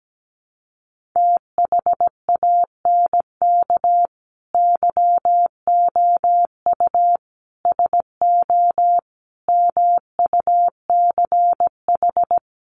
Morse code, heard as a keyed tone, about 17 wpm, 700 Hz. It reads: THANK YOU SO MUCH
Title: Sets of 5 Words 17wpm